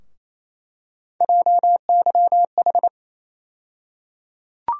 Morse code, heard as a keyed tone, about 28 wpm, 700 Hz. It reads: JYH
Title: USA Call Sign Postfixes - 3 Letters 28wpm